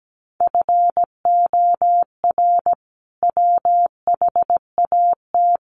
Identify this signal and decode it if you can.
Morse code, heard as a keyed tone, about 17 wpm, 700 Hz. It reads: FOR WHAT
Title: Sentences from Top 700 Words 17wpm